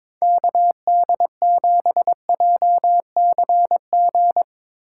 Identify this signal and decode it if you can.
Morse code, heard as a keyed tone, about 22 wpm, 700 Hz. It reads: KD7JCG
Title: USA Call Signs - 6 Characters - 8x Character Spacing - 22wpm